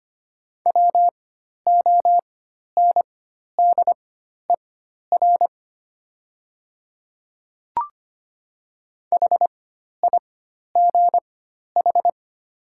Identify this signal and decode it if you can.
Morse code, heard as a keyed tone, about 25 wpm, spaced as for 12 wpm, 700 Hz. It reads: WONDER HIGH
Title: 4x Character Spacing - Top 500 Words - 25wpm